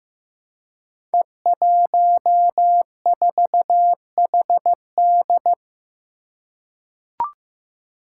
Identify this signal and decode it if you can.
Morse code, heard as a keyed tone, about 15 wpm, 700 Hz. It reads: E14HD
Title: Five Challenging Letter-Number Combinations 15wpm